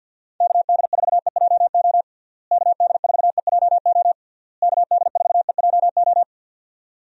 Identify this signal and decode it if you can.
Morse code, heard as a keyed tone, about 51 wpm, 700 Hz. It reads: KD4EJO KD4EJO KD4EJO
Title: USA Call Signs - 6 Characters - Repeated 3x - 50wpm